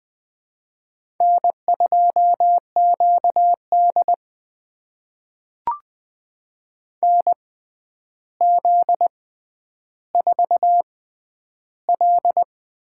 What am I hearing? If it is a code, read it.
Morse code, heard as a keyed tone, about 21 wpm, spaced as for 10 wpm, 700 Hz. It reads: N2QD NZ4L
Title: USA Call Signs - 4 Characters - 6x Character Spacing - 20wpm